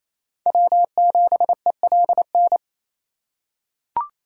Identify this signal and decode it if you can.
Morse code, heard as a keyed tone, about 28 wpm, 700 Hz. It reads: W7ELN
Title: USA Call Signs - 3x Character Spacing - 28wpm